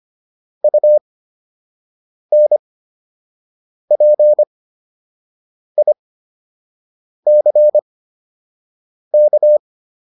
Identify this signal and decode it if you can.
Morse code, heard as a keyed tone, about 25 wpm, spaced as for 6 wpm, 600 Hz. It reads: UNPICK